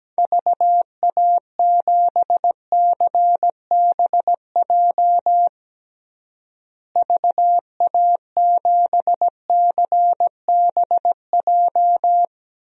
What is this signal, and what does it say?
Morse code, heard as a keyed tone, about 17 wpm, 700 Hz. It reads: VA7CBJ VA7CBJ
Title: DXCC-100 Call Signs - 6 Characters - Repeated 3x - 17wpm